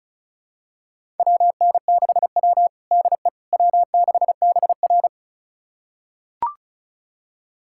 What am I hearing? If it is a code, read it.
Morse code, heard as a keyed tone, about 36 wpm, 700 Hz. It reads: WN6W DE W6BR
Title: USA Callsign DE Callsign - 4 Characters - 35wpm